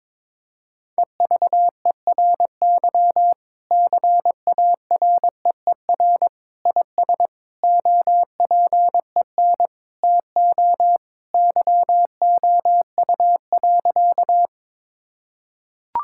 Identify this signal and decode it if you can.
Morse code, heard as a keyed tone, about 22 wpm, 700 Hz. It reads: EVERY CAREER IS OPEN TO YOU.
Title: Sentences from Top 1000 Words - Review Part 6 - 22wpm